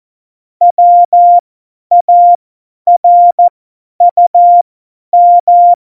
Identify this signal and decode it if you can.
Morse code, heard as a keyed tone, about 14 wpm, spaced as for 10 wpm, 700 Hz. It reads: WARUM
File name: Morsekurs / Witze / Witz 28. Mai 2019, 50 F70